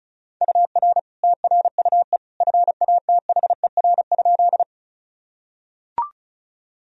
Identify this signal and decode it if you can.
Morse code, heard as a keyed tone, about 36 wpm, 700 Hz. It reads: UR TRUE FATHER?
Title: Sentences from Top 400 Words and Abbreviations 35wpm